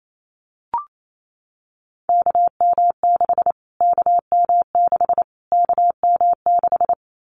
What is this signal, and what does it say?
Morse code, heard as a keyed tone, about 28 wpm, 700 Hz. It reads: KM6 KM6 KM6
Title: Callsign Prefixes for Top 25 DXCC - Repeated 3x - 28wpm